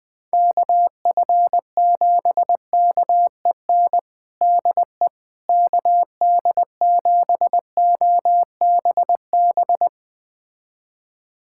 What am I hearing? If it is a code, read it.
Morse code, heard as a keyed tone, about 20 wpm, 700 Hz. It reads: KF7KEN DE KD7OBB
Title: USA Callsign DE Callsign - 6 Characters - 20wpm